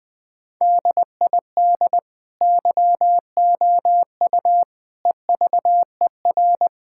Morse code, heard as a keyed tone, about 20 wpm, 700 Hz. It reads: DID YOU EVER
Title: Sentences from Top 1000 Words - Review Part 6 - 20wpm